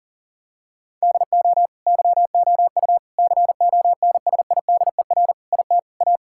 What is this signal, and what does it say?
Morse code, heard as a keyed tone, about 40 wpm, 700 Hz. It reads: DO YOU CONSIDER IT A
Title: Sentences from Top 1000 Words - Review Part 6 - 40wpm